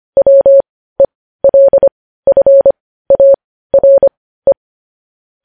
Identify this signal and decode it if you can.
Morse code, heard as a keyed tone, about 25 wpm, spaced as for 15 wpm, 550 Hz. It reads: WELFARE
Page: news headlines in morse code at 15 wpm